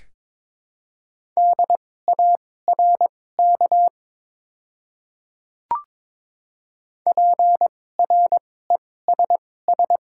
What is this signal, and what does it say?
Morse code, heard as a keyed tone, about 22 wpm, spaced as for 16 wpm, 700 Hz. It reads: DARK PRESS
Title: 2x Character Spacing - Top 500 Words - 22wpm